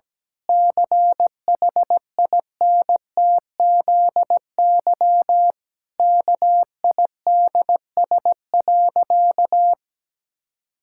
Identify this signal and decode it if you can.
Morse code, heard as a keyed tone, about 17 wpm, 700 Hz. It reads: CHINTZY KIDS.
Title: Pangram Challenge v2 - 17wpm